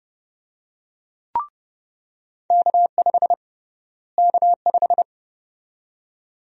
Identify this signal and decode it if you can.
Morse code, heard as a keyed tone, about 30 wpm, 700 Hz. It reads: K5 K5
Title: USA Call Sign Prefixes - Repeated 2x - 30wpm